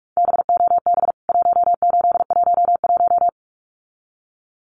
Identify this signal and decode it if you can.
Morse code, heard as a keyed tone, about 44 wpm, 700 Hz. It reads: BOB 1811
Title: CWT Exchange 45wpm